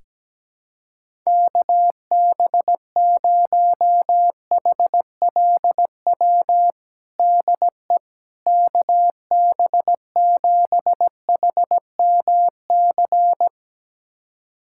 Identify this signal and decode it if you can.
Morse code, heard as a keyed tone, about 17 wpm, 700 Hz. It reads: KB0HLW DE KB7HMC
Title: USA Callsign DE Callsign - 6 Characters - 17wpm